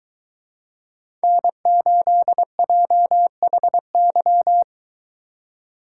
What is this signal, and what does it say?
Morse code, heard as a keyed tone, about 23 wpm, 700 Hz. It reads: N8JHY